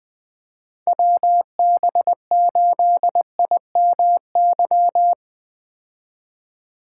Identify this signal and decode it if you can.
Morse code, heard as a keyed tone, about 20 wpm, 700 Hz. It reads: WB8IMY
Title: K1USN SST QSO Practice - 20wpm